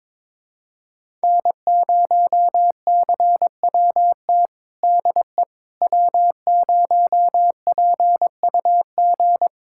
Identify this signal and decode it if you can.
Morse code, heard as a keyed tone, about 22 wpm, 700 Hz. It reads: N0CWT DE W0PUG
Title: USA Callsign DE Callsign - 5 Characters - 22wpm